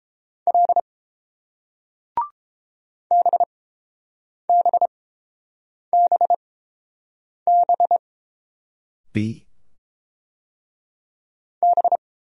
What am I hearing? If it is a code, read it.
Morse code, heard as a keyed tone, about 29 wpm, spaced as for 7 wpm, 690 Hz. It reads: L BBBB B